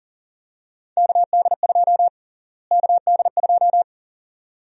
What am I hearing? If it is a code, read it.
Morse code, heard as a keyed tone, about 37 wpm, 700 Hz. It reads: KD2 KD2
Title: USA Call Sign Prefixes - Repeated 2x - 40wpm